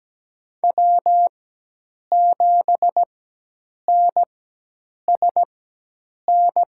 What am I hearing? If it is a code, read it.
Morse code, heard as a keyed tone, about 17 wpm, spaced as for 8 wpm, 700 Hz. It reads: W7NSN
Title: USA Call Signs - 4x Character Spacing - 17wpm